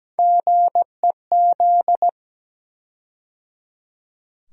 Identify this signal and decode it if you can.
Morse code, heard as a keyed tone, about 17 wpm, 700 Hz. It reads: GEZ
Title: USA Call Sign Postfixes - 3 Letters 17wpm